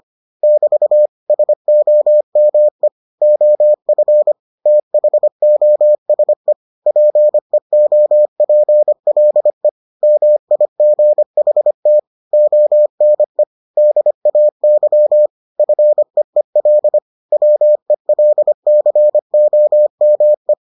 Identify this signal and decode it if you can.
Morse code, heard as a keyed tone, about 25 wpm, 600 Hz. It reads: <BT> SOME OF THOSE PEOPLE MIGHT ONE DAY FEEL WELCOME